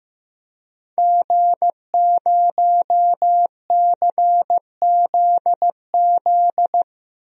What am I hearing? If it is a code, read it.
Morse code, heard as a keyed tone, about 15 wpm, 700 Hz. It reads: G0CZZ
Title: DXCC-100 Call Signs - 5 Characters - Repeated 3x - 15wpm